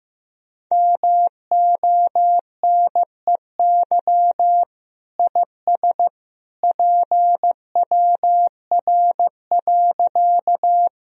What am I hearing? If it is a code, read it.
Morse code, heard as a keyed tone, about 15 wpm, 700 Hz. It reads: MONEY IS PWR.
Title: Sentences from Top 500 Words and Abbreviations 15wpm